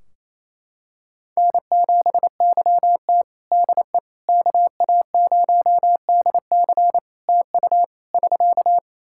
Morse code, heard as a keyed tone, about 28 wpm, 700 Hz. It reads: N7YT DE KA0DC TU <SK>
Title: Practice QSO - Exchange 4 - Ending - 28wpm